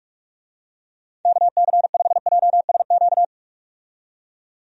Morse code, heard as a keyed tone, about 46 wpm, 700 Hz. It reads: KC5JSQ
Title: USA Call Signs - 6 Characters - 45wpm